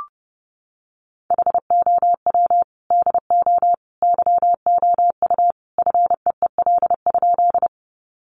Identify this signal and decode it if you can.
Morse code, heard as a keyed tone, about 30 wpm, 700 Hz. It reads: HOW DO YOU FEEL?
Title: Sentences from Top 400 Words 30wpm